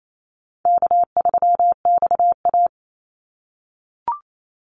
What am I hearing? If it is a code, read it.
Morse code, heard as a keyed tone, about 28 wpm, 700 Hz. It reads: K3XA
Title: Call Signs - 28wpm